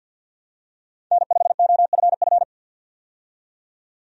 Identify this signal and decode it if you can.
Morse code, heard as a keyed tone, about 50 wpm, 700 Hz. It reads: N5OFF
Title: USA Call Signs - 5 Characters - 50wpm